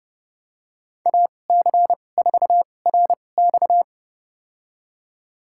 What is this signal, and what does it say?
Morse code, heard as a keyed tone, about 30 wpm, spaced as for 22 wpm, 700 Hz. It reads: AC4RX
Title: USA Call Signs - 5 Characters - 2x Character Spacing - 30wpm